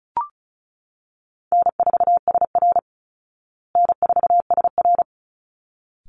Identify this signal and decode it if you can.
Morse code, heard as a keyed tone, about 34 wpm, 700 Hz. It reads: N4SR N4SR
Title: Call Signs Repeated 3x - 35wpm